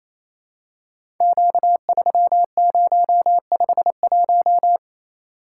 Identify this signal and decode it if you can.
Morse code, heard as a keyed tone, about 27 wpm, 700 Hz. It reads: Q3051
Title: Code Groups - 5 Letter:Numbers - 28wpm